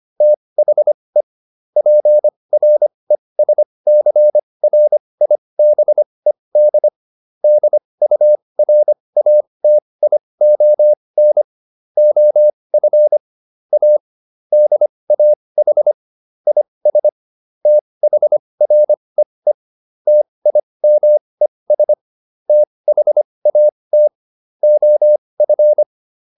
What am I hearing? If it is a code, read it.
Morse code, heard as a keyed tone, about 25 wpm, spaced as for 20 wpm, 600 Hz. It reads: THE PRESCRIBED DURATION OF A DAH IS THREE TIMES THAT OF
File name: SS316 20